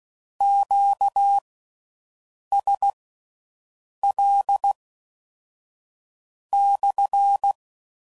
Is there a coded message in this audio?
Yes: Morse code from 0.4 to 7.5 s, heard as a keyed tone, about 16 wpm, spaced as for 6 wpm, 800 Hz. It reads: QSL /